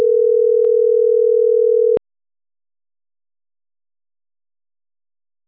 Ringback tone (North America), 440 + 480 Hz.